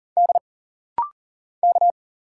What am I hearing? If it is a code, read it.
Morse code, heard as a keyed tone, about 40 wpm, 700 Hz. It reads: D K
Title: Single Letters - Rapid-Fire 40wpm